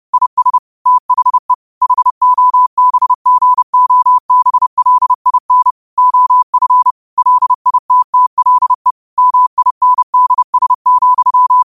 Morse code, heard as a keyed tone, about 30 wpm, 1000 Hz. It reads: IS THE HOBGOBLIN OF LITTLE MINDS,
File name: qotdfiles / 30wpm